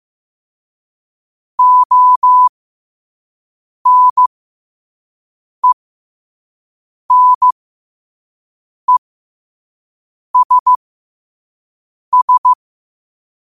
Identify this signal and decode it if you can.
Morse code, heard as a keyed tone, about 15 wpm, spaced as for 5 wpm, 1000 Hz. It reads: ONENESS
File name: qotdfiles / 5wpm